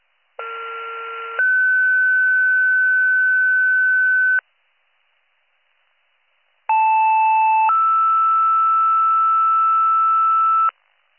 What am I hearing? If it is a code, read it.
Two-tone page, heard as a one-second tone, then a three-second tone, about 1370 then 1500 Hz, about 880 then 1320 Hz. It is heard from 0.4 to 10.7 s.